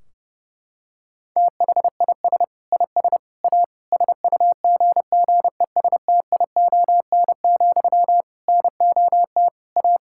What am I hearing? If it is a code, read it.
Morse code, heard as a keyed tone, about 30 wpm, 700 Hz. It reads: THIS IS A SUGGESTION, NOT A
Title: Sentences from Top 1600 Words 30wpm